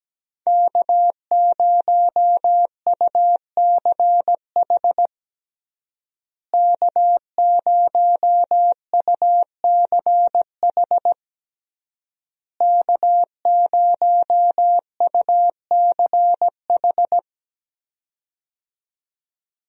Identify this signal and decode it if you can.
Morse code, heard as a keyed tone, about 17 wpm, 700 Hz. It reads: K0UCH K0UCH K0UCH
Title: USA Call Signs - 5 Characters - Repeated 3x - 17wpm